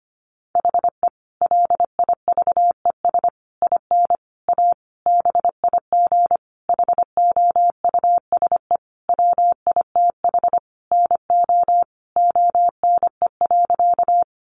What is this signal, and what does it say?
Morse code, heard as a keyed tone, about 25 wpm, 700 Hz. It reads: HE LIVES IN A BIG HOUSE WITH NO ONE.
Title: Additional Easy Phrases 25wpm